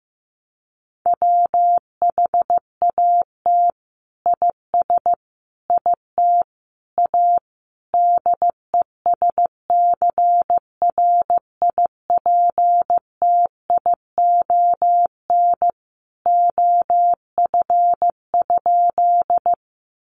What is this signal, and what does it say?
Morse code, heard as a keyed tone, about 15 wpm, 700 Hz. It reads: WHAT IS IT A DESCRIPTION OF?
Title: Sentences from Top 1300 Words 15wpm